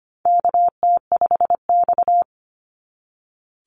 Morse code, heard as a keyed tone, about 25 wpm, 700 Hz. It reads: KT5X
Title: NA SOTA Chaser Callsigns 25wpm